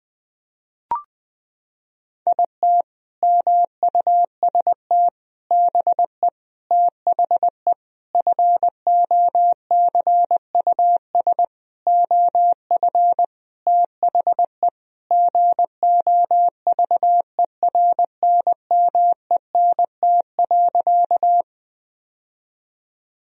Morse code, heard as a keyed tone, about 21 wpm, 700 Hz. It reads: IT MUST BE THE FOCUS OF THE GOVERNMENT.
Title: Sentences from Top 1000 Words - Review Part 4 - 20wpm